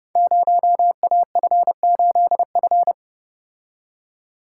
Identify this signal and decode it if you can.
Morse code, heard as a keyed tone, about 29 wpm, 700 Hz. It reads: 0AF8F
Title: Code Groups - 5 Letter:Numbers - 30wpm